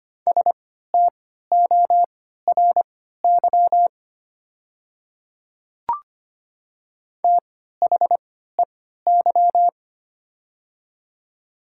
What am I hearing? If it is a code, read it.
Morse code, heard as a keyed tone, about 25 wpm, spaced as for 14 wpm, 700 Hz. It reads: STORY THEY
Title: 3x Character Spacing - Top 300 Words - 25wpm